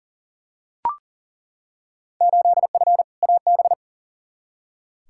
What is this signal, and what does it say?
Morse code, heard as a keyed tone, about 40 wpm, 700 Hz. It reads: 8F AB
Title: Field Day Exchanges Encore - 40wpm